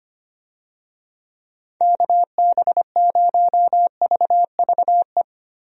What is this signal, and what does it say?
Morse code, heard as a keyed tone, about 25 wpm, 700 Hz. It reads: KB0VVE
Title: USA Call Signs - 6 Characters - 25wpm